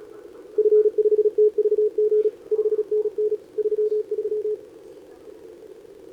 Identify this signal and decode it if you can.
Morse code, heard as a keyed tone, about 34 wpm, 410 Hz. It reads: F5TVG 5NN 33
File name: CQWWCW2025 / SobotaOdpol 10m CN3A